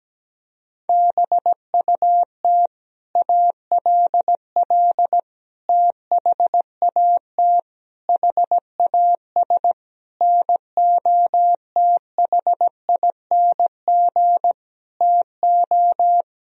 Morse code, heard as a keyed tone, about 17 wpm, 700 Hz. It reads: BUT ALL THAT HAS NOTHING TO
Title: Sentences from Top 1000 Words - Review Part 3 - 17wpm